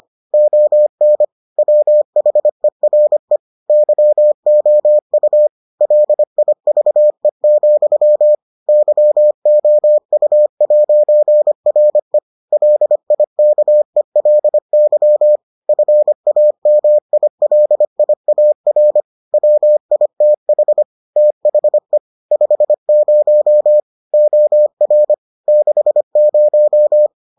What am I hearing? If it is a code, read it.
Morse code, heard as a keyed tone, about 25 wpm, 600 Hz. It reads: ON WHERE YOU LIVE, YOU'RE LIKELY FAMILIAR WITH THE 50 OR 60